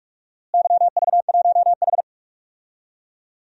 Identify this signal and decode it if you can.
Morse code, heard as a keyed tone, about 41 wpm, 700 Hz. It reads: YV1H
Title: DXCC-100 Call Signs - 4 Characters - Repeated 2x - 45wpm